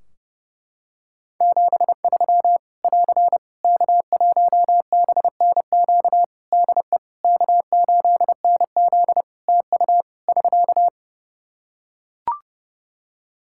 Morse code, heard as a keyed tone, about 30 wpm, 700 Hz. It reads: 73 <AR> K1BNQ DE K8NZ TU <SK>
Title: Practice QSO - Exchange 4 - Ending - 30wpm